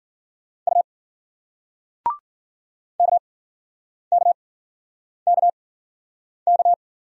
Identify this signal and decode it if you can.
Morse code, heard as a keyed tone, about 56 wpm, spaced as for 9 wpm, 700 Hz. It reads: U KKKK